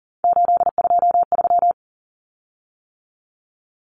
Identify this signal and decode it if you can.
Morse code, heard as a keyed tone, about 40 wpm, 700 Hz. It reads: 823
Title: Common Cut Numbers - 3-Digit 40wpm